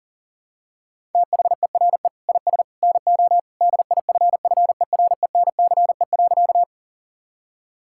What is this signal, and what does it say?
Morse code, heard as a keyed tone, about 40 wpm, 700 Hz. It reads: THERE IS NO DIFFERENCE.
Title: Sentences from Top 1000 Words - Review Part 5 - 40wpm